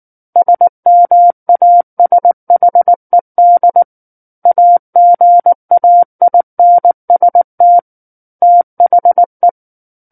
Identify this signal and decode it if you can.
Morse code, heard as a keyed tone, about 19 wpm, 700 Hz. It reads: SMASHED AGAINST THE
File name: Book One / 10-Chapter 10 - 19 WPM